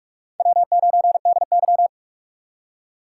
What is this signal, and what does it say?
Morse code, heard as a keyed tone, about 43 wpm, 700 Hz. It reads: W9DY
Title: K1USN SST QSO Practice - 45wpm